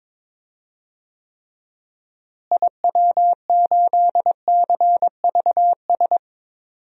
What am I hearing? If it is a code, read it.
Morse code, heard as a keyed tone, about 22 wpm, 700 Hz. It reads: IW8CVS
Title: DXCC-100 Call Signs - 6 Characters - Repeated 2x - 22wpm